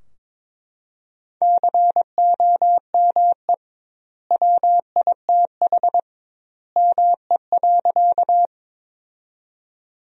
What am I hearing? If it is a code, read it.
Morse code, heard as a keyed tone, about 22 wpm, 700 Hz. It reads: COME WITH ME.
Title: 2x Word-Spacing - Sentences from Top 200 Words - 22wpm